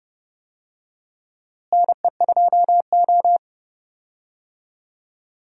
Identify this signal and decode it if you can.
Morse code, heard as a keyed tone, about 32 wpm, 700 Hz. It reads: NE2O
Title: USA Call Signs - 4 Characters - Repeated 3x - 30wpm